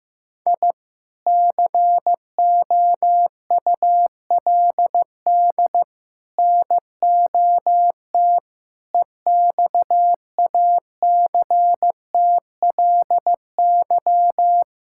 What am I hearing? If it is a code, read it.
Morse code, heard as a keyed tone, about 14 wpm, 700 Hz. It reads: I COULD NOT EXACTLY
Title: Sentences from Top 1000 Words - Review Part 6 - 15wpm